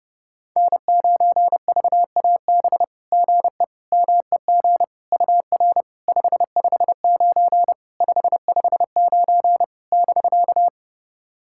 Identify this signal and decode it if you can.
Morse code, heard as a keyed tone, about 31 wpm, 700 Hz. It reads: N9VAB GE MEG UR 559 559 <BK>
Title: SOTA QSO Practice 30wpm